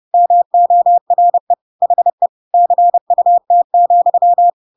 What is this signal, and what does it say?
Morse code, heard as a keyed tone, about 30 wpm, 700 Hz. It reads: MORE HE CUT,
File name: Pinocchio / Pinocchio ch3 30 30 ik1hge